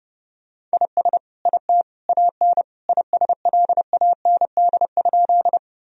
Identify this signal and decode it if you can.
Morse code, heard as a keyed tone, about 30 wpm, 700 Hz. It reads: IS IT AN ISLAND?
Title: Sentences from Top 1000 Words - Review Part 3 - 30wpm